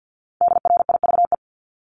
Morse code, heard as a keyed tone, about 50 wpm, 690 Hz. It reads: DRIVE
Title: Multi-letter Phonemes 50wpm